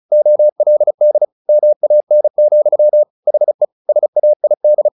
Morse code, heard as a keyed tone, about 35 wpm, 600 Hz. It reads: OLD MAN, HE SAID
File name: the old man and the sea 2 35wpm